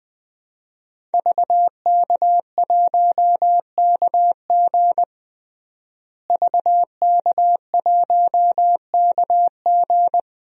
Morse code, heard as a keyed tone, about 20 wpm, 700 Hz. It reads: VK1KG VK1KG
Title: DXCC-100 Call Signs - 5 Characters - Repeated 2x - 20wpm